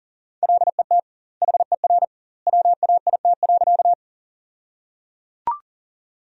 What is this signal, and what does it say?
Morse code, heard as a keyed tone, about 37 wpm, 700 Hz. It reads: LET HER WAIT.
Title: 2x Word-Spacing - Sentences from Top 500 Words - 40wpm